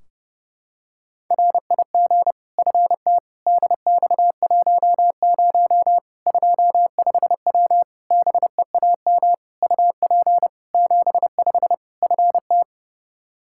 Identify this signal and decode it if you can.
Morse code, heard as a keyed tone, about 32 wpm, 700 Hz. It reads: RIG FT DX10 25W BEAM UP 75 FT
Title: Practice QSO - Exchange 2 - Info - 30wpm